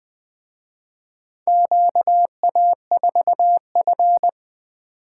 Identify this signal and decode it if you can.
Morse code, heard as a keyed tone, about 24 wpm, 700 Hz. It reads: QA4F
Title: Speed-Racing - Four Letter-Number Combinations - 20wpm